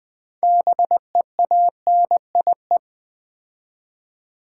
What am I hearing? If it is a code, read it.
Morse code, heard as a keyed tone, about 21 wpm, 700 Hz. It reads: BEANIE
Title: Difficult Words with Hints - 20wpm